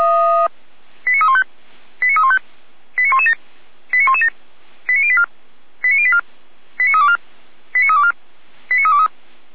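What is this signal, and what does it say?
Five-tone selective call (ZVEI), heard as quick runs of five tones, tones of about 70 ms. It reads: 89315 89315 89117 89117 89084 89084 89324 89324 89323
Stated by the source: ZVEI five-tone selective call received by SDR.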